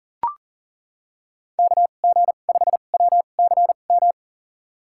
Morse code, heard as a keyed tone, about 40 wpm, spaced as for 29 wpm, 700 Hz. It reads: KG5WCM
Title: USA Call Signs - 2x Character Spacing - 40wpm